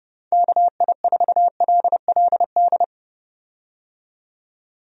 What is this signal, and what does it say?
Morse code, heard as a keyed tone, about 30 wpm, 700 Hz. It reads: KI4LLD
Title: USA Call Signs - 6 Characters - 30wpm